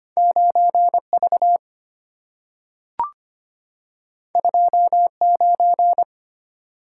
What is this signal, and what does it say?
Morse code, heard as a keyed tone, about 25 wpm, 700 Hz. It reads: 9V 29
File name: Two Letter-Number Combinations - 25wpm